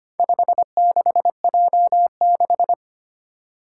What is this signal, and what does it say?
Morse code, heard as a keyed tone, about 25 wpm, 700 Hz. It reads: 56J6
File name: practice / 25wpm / Four Letter-Number Combinations - 25wpm